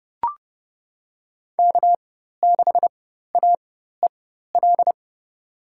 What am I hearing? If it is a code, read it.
Morse code, heard as a keyed tone, about 29 wpm, spaced as for 14 wpm, 700 Hz. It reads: K6AEL
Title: USA Call Signs - 5 Characters - 4x Character Spacing - 30wpm